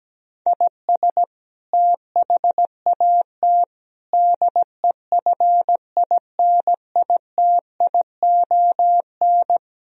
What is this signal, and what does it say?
Morse code, heard as a keyed tone, about 17 wpm, 700 Hz. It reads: IS THAT DEFINITION
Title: Sentences from Top 1000 Words - Review Part 4 - 17wpm